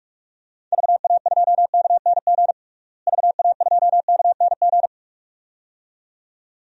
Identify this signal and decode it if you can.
Morse code, heard as a keyed tone, about 46 wpm, 700 Hz. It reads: VA2KNG VA2KNG
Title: DXCC-100 Call Signs - 6 Characters - Repeated 2x - 45wpm